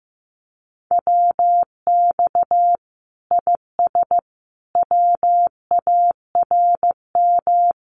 Morse code, heard as a keyed tone, about 15 wpm, 700 Hz. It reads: WX IS WARM
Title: Common QSO Phrases 15wpm